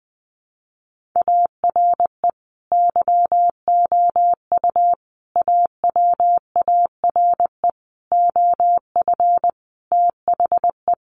Morse code, heard as a keyed tone, about 20 wpm, 700 Hz. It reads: ARE YOU AWARE OF THE
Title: Sentences from Top 1500 Words 20wpm